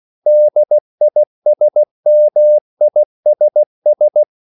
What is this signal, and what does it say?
Morse code, heard as a keyed tone, about 16 wpm, 600 Hz. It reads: DISMISS
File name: CW / Practice / EnglishQuotations0416